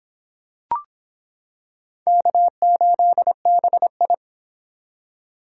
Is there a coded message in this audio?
Yes: Morse code from 2.1 to 4.1 s, heard as a keyed tone, about 26 wpm, 700 Hz. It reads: K8BI